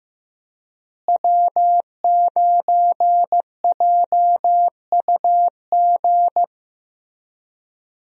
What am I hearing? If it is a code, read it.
Morse code, heard as a keyed tone, about 14 wpm, 700 Hz. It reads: W9JUG
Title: USA Call Signs - 5 Characters - Repeated 3x - 15wpm